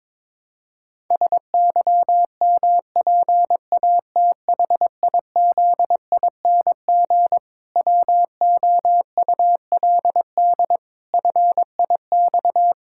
Morse code, heard as a keyed tone, about 22 wpm, 700 Hz. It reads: SYMPATHIZING WOULD FIX
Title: Pangram Challenge v2 - 22wpm